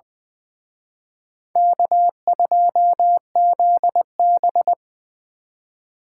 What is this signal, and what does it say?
Morse code, heard as a keyed tone, about 23 wpm, 700 Hz. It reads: K2ZB
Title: Speed Racing - USA 1x2 Call Signs 20wpm